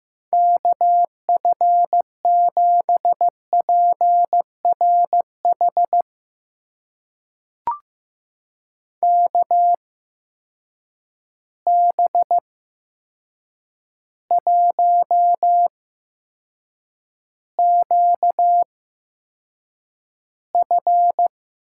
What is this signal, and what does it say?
Morse code, heard as a keyed tone, about 15 wpm, spaced as for 7 wpm, 700 Hz. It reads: KF7PRH KB1QF
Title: USA Call Signs - 6 Characters - 8x Character Spacing - 15wpm